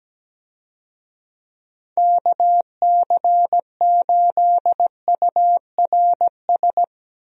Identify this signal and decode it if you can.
Morse code, heard as a keyed tone, about 17 wpm, 700 Hz. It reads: KC8URS
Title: USA Call Signs - 6 Characters - Repeated 3x - 17wpm